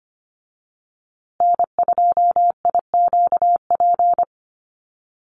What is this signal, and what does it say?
Morse code, heard as a keyed tone, about 25 wpm, 700 Hz. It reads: N2IQP
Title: USA Call Signs Repeated 3x 25wpm